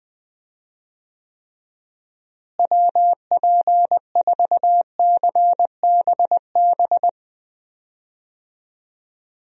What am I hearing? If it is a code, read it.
Morse code, heard as a keyed tone, about 20 wpm, 700 Hz. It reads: WP4CBB
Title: USA Call Signs - 6 Characters - Repeated 3x - 20wpm